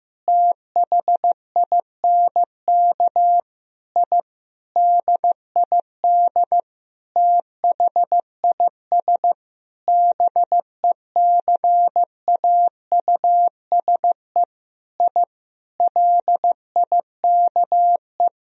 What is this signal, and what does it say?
Morse code, heard as a keyed tone, about 15 wpm, 700 Hz. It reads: THINK I DID THIS BECAUSE I LIKE
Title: Sentences from Top 1000 Words - Review Part 3 - 15wpm